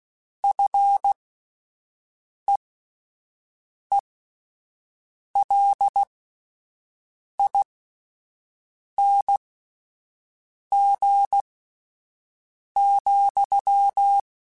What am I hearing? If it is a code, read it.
Morse code, heard as a keyed tone, about 16 wpm, spaced as for 5 wpm, 800 Hz. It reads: FEELING,